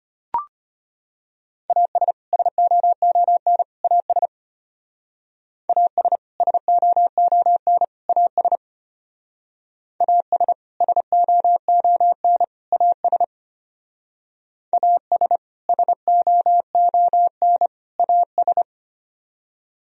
Morse code, heard as a keyed tone, about 31 wpm, 700 Hz. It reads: AS SOON AS AS SOON AS AS SOON AS AS SOON AS
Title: Speed-Racing - Sets of 3 Words 25wpm